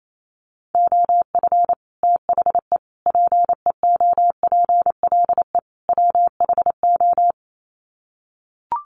Morse code, heard as a keyed tone, about 28 wpm, 700 Hz. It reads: OF THE PEOPLE WHO
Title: Sets of 4 Words 28wpm